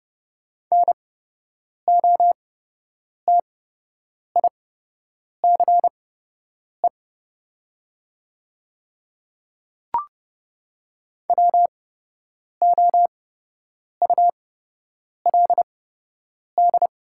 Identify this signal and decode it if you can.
Morse code, heard as a keyed tone, about 30 wpm, spaced as for 8 wpm, 700 Hz. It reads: NOTICE WOULD
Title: 8x Character Spacing - Top 500 Words - 30wpm